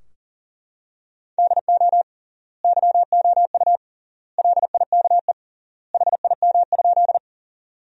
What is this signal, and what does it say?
Morse code, heard as a keyed tone, about 40 wpm, 700 Hz. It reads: DO YOU LIKE HIM?
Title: 3x Word-Spacing - Sentences from Top 100 Words - 40wpm